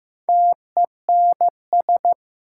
Morse code, heard as a keyed tone, about 15 wpm, 700 Hz. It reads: TENS
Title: Sets of 4 Words - New Word Drill - 15wpm